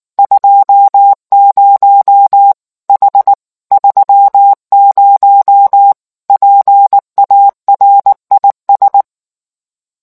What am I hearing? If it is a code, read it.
Morse code, heard as a keyed tone, about 19 wpm, 800 Hz. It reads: 20 H 30 PARIS